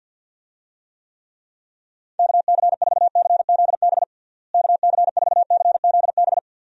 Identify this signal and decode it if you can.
Morse code, heard as a keyed tone, about 48 wpm, 700 Hz. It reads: KC4CZB KC4CZB
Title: USA Call Signs - 6 Characters - Repeated 2x - 50wpm